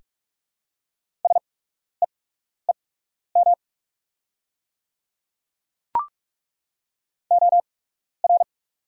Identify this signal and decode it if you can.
Morse code, heard as a keyed tone, about 46 wpm, spaced as for 12 wpm, 690 Hz. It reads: SEEM OR
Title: 8x Character Spacing - Top 300 Words - 45wpm